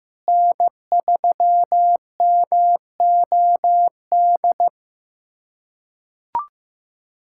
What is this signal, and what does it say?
Morse code, heard as a keyed tone, about 16 wpm, 700 Hz. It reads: N3MOD
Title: USA Callsign DE Callsign - 5 Characters - 15wpm